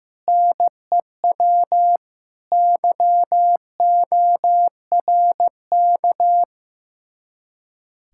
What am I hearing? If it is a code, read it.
Morse code, heard as a keyed tone, about 15 wpm, 700 Hz. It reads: NEW YORK
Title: Speed-Racing Common QSO Phrases Encore - 15wpm